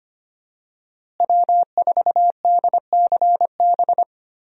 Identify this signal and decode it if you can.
Morse code, heard as a keyed tone, about 25 wpm, 700 Hz. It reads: W4DCB
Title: USA Call Signs - 5 Characters - Repeated 3x - 25wpm